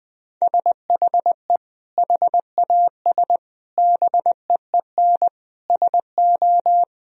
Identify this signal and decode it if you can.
Morse code, heard as a keyed tone, about 21 wpm, 700 Hz. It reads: SHE HAS BEEN SO
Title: Sentences from Top 1000 Words - Review Part 2 - 20wpm